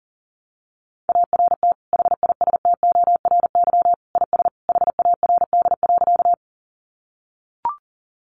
Morse code, heard as a keyed tone, about 40 wpm, 700 Hz. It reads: ART HISTORY IS HARD.
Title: Sentences from Top 600 Words 40wpm